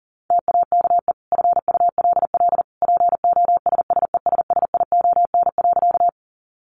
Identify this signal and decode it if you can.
Morse code, heard as a keyed tone, about 40 wpm, 700 Hz. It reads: TAKE FULL POSSESSION.
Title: Sentences from Top 1400 Words 40wpm